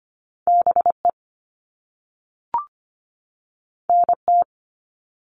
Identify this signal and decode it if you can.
Morse code, heard as a keyed tone, about 25 wpm, 700 Hz. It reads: BE NT
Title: All Cut Numbers - 2-Digit 25wpm